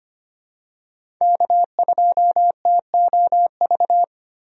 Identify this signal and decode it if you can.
Morse code, heard as a keyed tone, about 25 wpm, 700 Hz. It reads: K2TOV
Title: USA Call Signs - 5 Characters - 25wpm